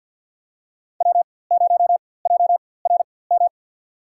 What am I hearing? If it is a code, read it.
Morse code, heard as a keyed tone, about 47 wpm, spaced as for 23 wpm, 700 Hz. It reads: W0JRM
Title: USA Call Signs - 5 Characters - 4x Character Spacing - 50wpm